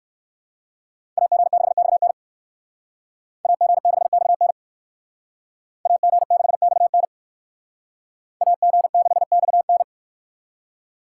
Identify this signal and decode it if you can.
Morse code, heard as a keyed tone, about 57 wpm, 700 Hz. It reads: AG6XN AG6XN AG6XN AG6XN